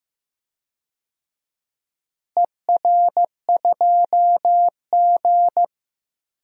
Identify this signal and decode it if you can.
Morse code, heard as a keyed tone, about 15 wpm, 700 Hz. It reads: ER2G
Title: DXCC-100 Call Signs - 4 Characters - Repeated 2x - 15wpm